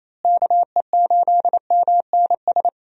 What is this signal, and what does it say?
Morse code, heard as a keyed tone, about 28 wpm, 700 Hz. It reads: KE8MNS
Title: USA Call Signs - 6 Characters - Repeated 3x - 28wpm